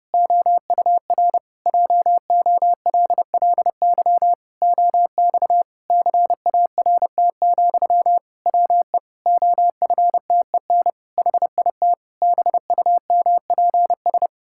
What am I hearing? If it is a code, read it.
Morse code, heard as a keyed tone, about 30 wpm, 700 Hz. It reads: OUR JOLLY OX CART, WE OFTEN HIT BUMPS
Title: Pangram Challenge v2 - 30wpm